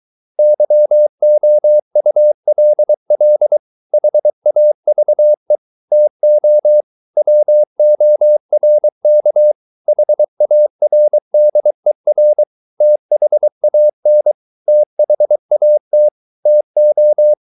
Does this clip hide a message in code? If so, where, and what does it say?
Yes: Morse code from 0.4 to 17.3 s, heard as a keyed tone, about 23 wpm, 600 Hz. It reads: YOULL HAVE TO WORK HARDER THAN THAT TO